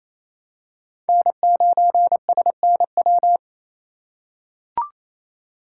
Morse code, heard as a keyed tone, about 28 wpm, 700 Hz. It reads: N9SNW
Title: USA Call Signs - 8x Character Spacing - 28wpm